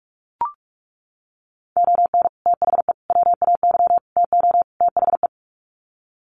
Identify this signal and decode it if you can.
Morse code, heard as a keyed tone, about 44 wpm, 700 Hz. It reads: ON THE WAY TO THE
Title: Sets of 5 Words 45wpm